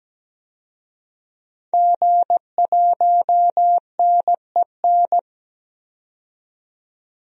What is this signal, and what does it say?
Morse code, heard as a keyed tone, about 17 wpm, 700 Hz. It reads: G1NEN
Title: DXCC-100 Call Signs - 5 Characters - 17wpm